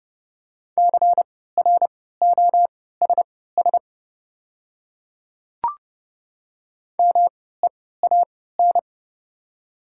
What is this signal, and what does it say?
Morse code, heard as a keyed tone, about 29 wpm, spaced as for 17 wpm, 700 Hz. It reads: CROSS MEAN
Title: 3x Character Spacing - Top 300 Words - 30wpm